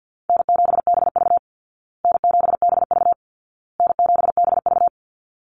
Morse code, heard as a keyed tone, about 50 wpm, 700 Hz. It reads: N7BV N7BV N7BV
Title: Easy Call Signs Repeated 3x - 50wpm